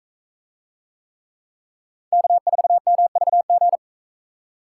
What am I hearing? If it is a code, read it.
Morse code, heard as a keyed tone, about 41 wpm, 700 Hz. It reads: K4MVG